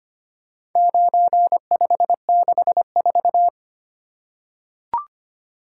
Morse code, heard as a keyed tone, about 25 wpm, 700 Hz. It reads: 9564
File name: practice / 25wpm / Four Letter-Number Combinations - 25wpm